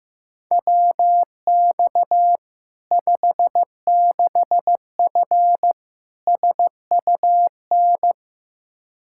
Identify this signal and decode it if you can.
Morse code, heard as a keyed tone, about 15 wpm, 700 Hz. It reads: WX 56F SUN
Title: Practice QSO - Exchange 2 - Info - 15wpm